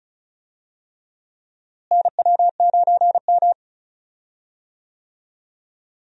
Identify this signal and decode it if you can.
Morse code, heard as a keyed tone, about 36 wpm, 700 Hz. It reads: NW9M
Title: USA Call Signs - 4 Characters - Repeated 3x - 35wpm